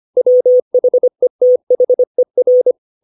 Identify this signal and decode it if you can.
Morse code, heard as a keyed tone, about 25 wpm, 500 Hz. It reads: WHETHER
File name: 632 words / 632 Words 25WPM